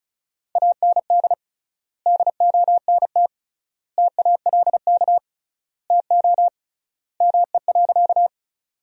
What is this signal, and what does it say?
Morse code, heard as a keyed tone, about 36 wpm, 700 Hz. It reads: AND DONT TALK TO ME.
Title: 3x Word-Spacing - Sentences from Top 400 Words - 35wpm